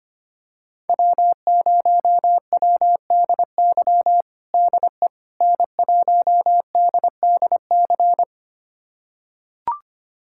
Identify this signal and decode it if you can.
Morse code, heard as a keyed tone, about 25 wpm, 700 Hz. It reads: W0WDY DE N1DDC
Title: USA Callsign DE Callsign - 5 Characters - 25wpm